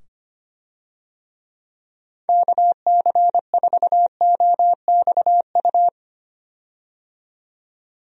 Morse code, heard as a keyed tone, about 25 wpm, 700 Hz. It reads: KC4OXU
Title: USA Call Signs - 6 Characters - 25wpm